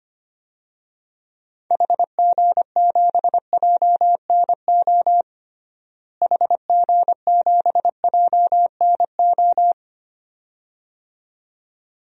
Morse code, heard as a keyed tone, about 25 wpm, 700 Hz. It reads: HG7JNO HG7JNO
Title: DXCC-100 Call Signs - 6 Characters - Repeated 2x - 25wpm